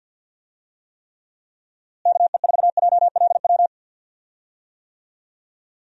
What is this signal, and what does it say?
Morse code, heard as a keyed tone, about 46 wpm, 700 Hz. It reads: KE4JLW
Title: USA Call Signs - 6 Characters - Repeated 2x - 50wpm